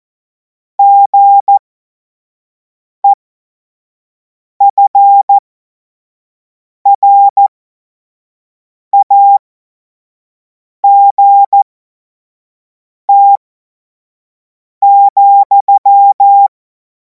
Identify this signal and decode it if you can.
Morse code, heard as a keyed tone, about 14 wpm, spaced as for 5 wpm, 800 Hz. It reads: GEFRAGT,
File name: Morsekurs / Witze / Witz 02. Juni 2026 25 F70